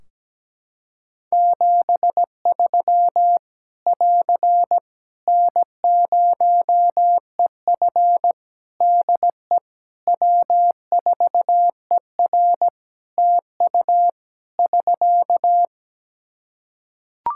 Morse code, heard as a keyed tone, about 17 wpm, 700 Hz. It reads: 73 <AR> N0EF DE W4ER TU <SK>
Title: Practice QSO - Exchange 4 - Ending - 17wpm